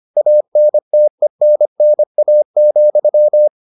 Morse code, heard as a keyed tone, about 25 wpm, 600 Hz. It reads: ANTENNA,